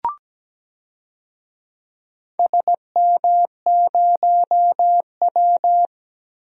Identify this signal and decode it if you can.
Morse code, heard as a keyed tone, about 17 wpm, 700 Hz. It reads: SM0W
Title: DXCC-100 Call Signs - 4 Characters - Repeated 3x - 17wpm